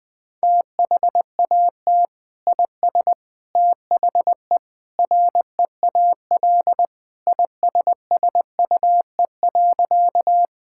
Morse code, heard as a keyed tone, about 21 wpm, 700 Hz. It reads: THAT IS THE REAL ISSUE.
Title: Sentences from Top 1000 Words - Review Part 6 - 20wpm